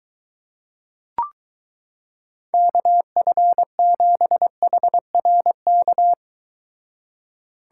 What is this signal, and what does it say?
Morse code, heard as a keyed tone, about 23 wpm, 700 Hz. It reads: KF7HRK